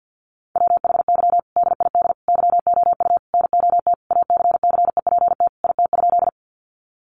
Morse code, heard as a keyed tone, about 50 wpm, 700 Hz. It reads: WHY DID YOU NOT ACCEPT IT?
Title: Sentences from Top 1300 Words 50wpm